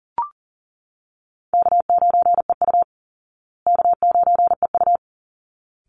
Morse code, heard as a keyed tone, about 40 wpm, 700 Hz. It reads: K9EU K9EU
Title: Call Signs Repeated 3x - 40wpm